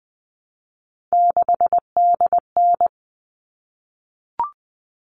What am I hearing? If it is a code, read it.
Morse code, heard as a keyed tone, about 20 wpm, 700 Hz. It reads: 6DN
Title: All Cut Numbers - 3-Digit 20wpm